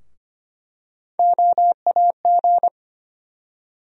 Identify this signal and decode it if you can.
Morse code, heard as a keyed tone, about 25 wpm, 700 Hz. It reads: OAG
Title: USA Call Sign Postfixes - 3 Letters 25wpm